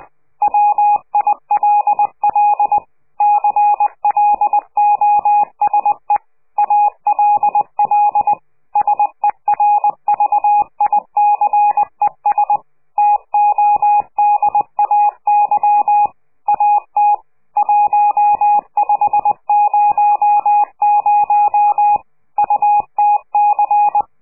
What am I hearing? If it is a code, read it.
Morse code, heard as a keyed tone, about 20 wpm, 830 Hz. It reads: WILL CLOSE ALL SERVICES TODAY AT 1500 UTC
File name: CQ de PCH laatste bericht farewell